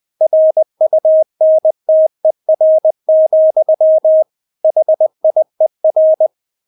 Morse code, heard as a keyed tone, about 20 wpm, 630 Hz. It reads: RUNTER, HIER
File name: Witz1 20wpm